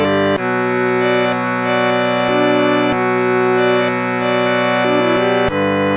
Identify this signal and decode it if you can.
There is no code.